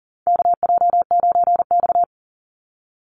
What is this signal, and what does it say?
Morse code, heard as a keyed tone, about 40 wpm, 700 Hz. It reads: KJ9X
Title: NA SOTA Chaser Callsigns 40wpm